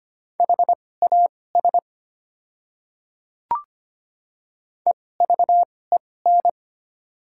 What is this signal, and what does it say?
Morse code, heard as a keyed tone, about 25 wpm, spaced as for 18 wpm, 700 Hz. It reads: HAS EVEN
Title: 2x Character Spacing - Top 200 Words - 25wpm